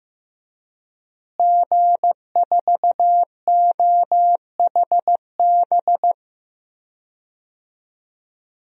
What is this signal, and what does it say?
Morse code, heard as a keyed tone, about 15 wpm, 700 Hz. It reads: G4OHB
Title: DXCC-100 Call Signs - 5 Characters - Repeated 2x - 15wpm